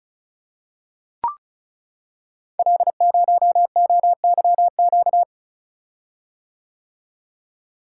Morse code, heard as a keyed tone, about 35 wpm, 690 Hz. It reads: L0OYQ
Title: Code Groups - 5 Letter:Numbers - 35wpm